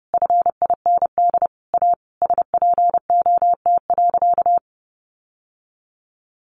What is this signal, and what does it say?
Morse code, heard as a keyed tone, about 30 wpm, 700 Hz. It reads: FIND A SPOT.
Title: Sentences from Top 1100 Words 30wpm